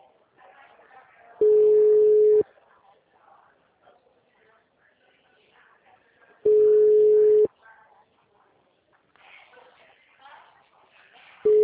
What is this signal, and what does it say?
Ringback tone (Europe, 425 Hz), with speech.
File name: out- -2036